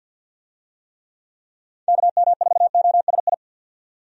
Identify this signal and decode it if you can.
Morse code, heard as a keyed tone, about 50 wpm, 700 Hz. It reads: KM4OSI
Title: USA Call Signs - 6 Characters - 50wpm